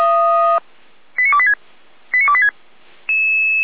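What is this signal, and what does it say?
Five-tone selective call (ZVEI), heard as quick runs of five tones, tones of about 70 ms. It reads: 89286 89286
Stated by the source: Five-tone ZVEI selective call received on an SDR.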